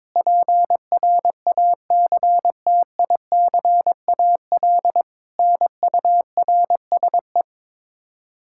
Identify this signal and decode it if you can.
Morse code, heard as a keyed tone, about 22 wpm, 700 Hz. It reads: PRACTICAL NURSE
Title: Practice QSO - Exchange 3 - More Info - 22wpm